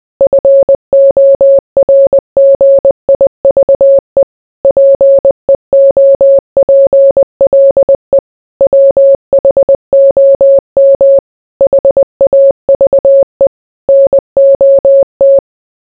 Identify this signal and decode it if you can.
Morse code, heard as a keyed tone, about 20 wpm, 550 Hz. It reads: FORGIVE PEOPLE WHOM HAVE NOT